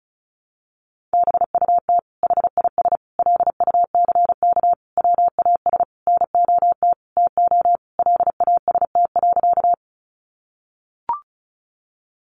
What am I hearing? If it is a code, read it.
Morse code, heard as a keyed tone, about 34 wpm, 700 Hz. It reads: BUT HIS LUCK WAS NOT TO LAST.
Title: Sentences from Top 1500 Words 35wpm